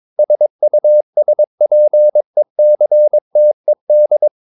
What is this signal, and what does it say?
Morse code, heard as a keyed tone, about 22 wpm, 600 Hz. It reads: SUSPECTED
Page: Romanen Never af Ken Follet på engelsk.